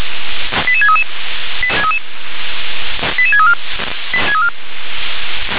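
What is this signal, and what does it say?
Five-tone selective call (ZVEI), heard as quick runs of five tones, tones of about 70 ms. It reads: 88633 88634
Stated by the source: ZVEI five-tone selective call received by SDR.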